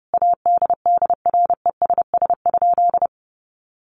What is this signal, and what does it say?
Morse code, heard as a keyed tone, about 30 wpm, 700 Hz. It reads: ADDRESS?
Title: Sentences from Top 1000 Words 30wpm